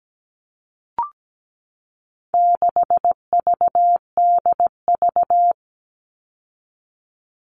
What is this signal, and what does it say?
Morse code, heard as a keyed tone, about 17 wpm, 700 Hz. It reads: 6VDV
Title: All Cut Numbers - 4-Digit 17wpm